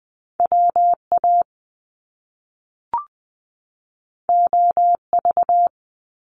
Morse code, heard as a keyed tone, about 20 wpm, 700 Hz. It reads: W A O V
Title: Top Two Letter Combinations in Words 20wpm